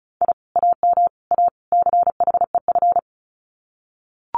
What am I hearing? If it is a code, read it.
Morse code, heard as a keyed tone, about 34 wpm, 700 Hz. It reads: I AM A CHEF
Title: Common QSO Phrases Encore 35wpm